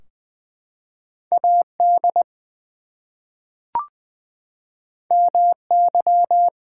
Morse code, heard as a keyed tone, about 20 wpm, 700 Hz. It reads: AD MY
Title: Two Letter Words - 20wpm